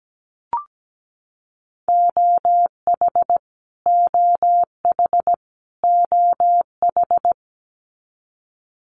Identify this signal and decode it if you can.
Morse code, heard as a keyed tone, about 17 wpm, 700 Hz. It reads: OH OH OH
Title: US States Repeated 3x 17wpm